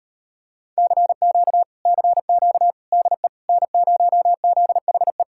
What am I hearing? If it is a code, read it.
Morse code, heard as a keyed tone, about 33 wpm, 700 Hz. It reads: CQ CQ DE N0ZHE